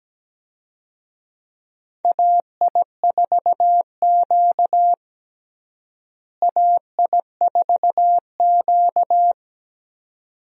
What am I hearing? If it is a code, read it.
Morse code, heard as a keyed tone, about 17 wpm, 700 Hz. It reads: AI4Q AI4Q
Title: USA Call Signs - 4 Characters - Repeated 2x - 17wpm